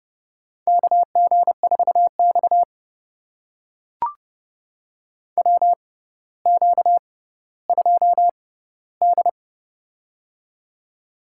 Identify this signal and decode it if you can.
Morse code, heard as a keyed tone, about 31 wpm, spaced as for 15 wpm, 700 Hz. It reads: KG4X WQ2D
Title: USA Call Signs - 4 Characters - 6x Character Spacing - 30wpm